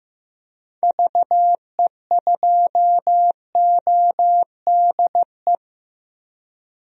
Morse code, heard as a keyed tone, about 15 wpm, 700 Hz. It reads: VE2ODE
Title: DXCC-100 Call Signs - 6 Characters - Repeated 2x - 15wpm